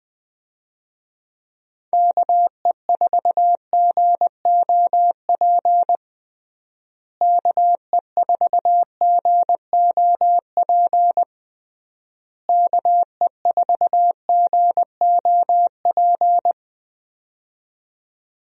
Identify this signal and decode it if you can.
Morse code, heard as a keyed tone, about 20 wpm, 700 Hz. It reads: KE4GOP KE4GOP KE4GOP
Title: USA Call Signs - 6 Characters - Repeated 3x - 20wpm